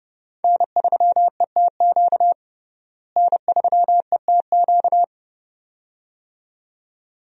Morse code, heard as a keyed tone, about 30 wpm, 700 Hz. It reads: N3ETQ N3ETQ
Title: USA Call Signs - 5 Characters - Repeated 2x - 30wpm